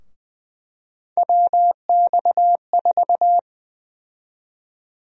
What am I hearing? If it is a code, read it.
Morse code, heard as a keyed tone, about 20 wpm, 700 Hz. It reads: WX4
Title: USA Call Sign Prefixes - 20wpm